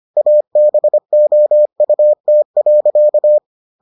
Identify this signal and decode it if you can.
Morse code, heard as a keyed tone, about 25 wpm, 600 Hz. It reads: ABOUT.